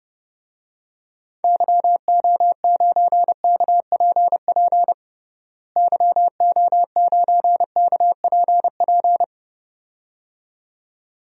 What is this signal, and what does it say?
Morse code, heard as a keyed tone, about 32 wpm, 700 Hz. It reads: YO9KPP YO9KPP
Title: DXCC-100 Call Signs - 6 Characters - Repeated 2x - 30wpm